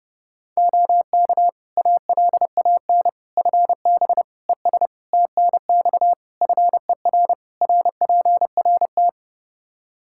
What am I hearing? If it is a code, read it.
Morse code, heard as a keyed tone, about 30 wpm, 700 Hz. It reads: OK ALAN FB ES TNX FER RPRT
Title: Practice QSO - Exchange 2 - Info - 30wpm